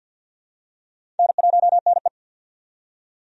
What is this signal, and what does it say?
Morse code, heard as a keyed tone, about 50 wpm, 700 Hz. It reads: N1NE
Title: USA Call Signs - 4 Characters - 50wpm